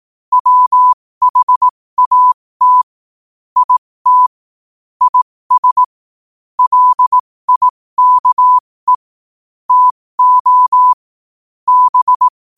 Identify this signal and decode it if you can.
Morse code, heard as a keyed tone, about 18 wpm, spaced as for 15 wpm, 1000 Hz. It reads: WHAT IT IS LIKE TO B
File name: qotdfiles / 15wpm